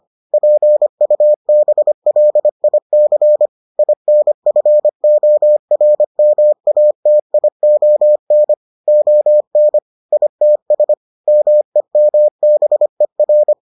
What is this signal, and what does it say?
Morse code, heard as a keyed tone, about 25 wpm, 600 Hz. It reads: PUBLIC INFORMATION ON ITS MEMBER